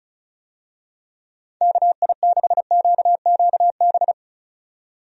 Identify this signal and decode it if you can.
Morse code, heard as a keyed tone, about 34 wpm, 700 Hz. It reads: KI6QQB
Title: USA Call Signs - 6 Characters - 35wpm